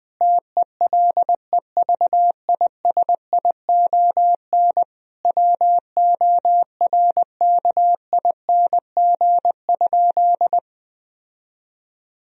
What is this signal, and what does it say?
Morse code, heard as a keyed tone, about 20 wpm, 700 Hz. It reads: TELEVISION WORKING?
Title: Sentences from Top 1000 Words - Review Part 4 - 20wpm